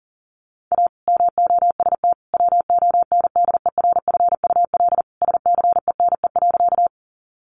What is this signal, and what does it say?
Morse code, heard as a keyed tone, about 40 wpm, 700 Hz. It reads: A MOST WONDERFUL SCENE.
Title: Sentences from Top 1100 Words 40wpm